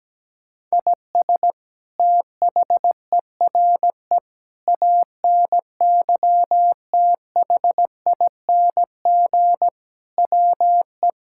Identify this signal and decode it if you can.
Morse code, heard as a keyed tone, about 17 wpm, 700 Hz. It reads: IS THERE ANYTHING WE
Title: Sentences from Top 1000 Words - Review Part 6 - 17wpm